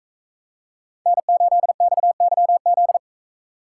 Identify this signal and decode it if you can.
Morse code, heard as a keyed tone, about 42 wpm, 700 Hz. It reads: N8XYZ